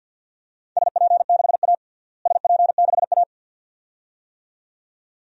Morse code, heard as a keyed tone, about 51 wpm, 700 Hz. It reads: SP6A SP6A
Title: DXCC-100 Call Signs - 4 Characters - Repeated 2x - 50wpm